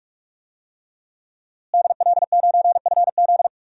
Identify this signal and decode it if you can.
Morse code, heard as a keyed tone, about 47 wpm, 700 Hz. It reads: DL9FZ
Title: DXCC-100 Call Signs - 5 Characters - Repeated 3x - 45wpm